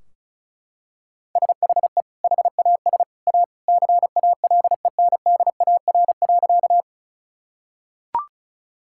Morse code, heard as a keyed tone, about 36 wpm, 700 Hz. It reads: SHE HAS A CALENDAR.
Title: Sentences from Top 1600 Words 35wpm